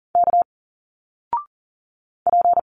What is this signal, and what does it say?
Morse code, heard as a keyed tone, about 40 wpm, 700 Hz. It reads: K P
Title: Single Letters 40wpm